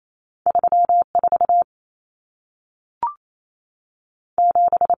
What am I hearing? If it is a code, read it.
Morse code, heard as a keyed tone, about 28 wpm, 700 Hz. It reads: 34 7
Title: Numbers - 2-Digit 28wpm